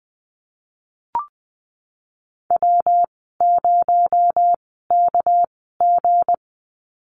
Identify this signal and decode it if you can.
Morse code, heard as a keyed tone, about 19 wpm, spaced as for 14 wpm, 700 Hz. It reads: W0KG
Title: 219 Lesson 35g - Number 0 - Call Signs 20wpm